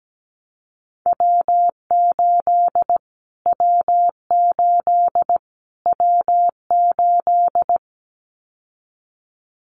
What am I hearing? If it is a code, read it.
Morse code, heard as a keyed tone, about 17 wpm, 700 Hz. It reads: W8 W8 W8
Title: Callsign Prefixes for Top 25 DXCC - Repeated 3x - 17wpm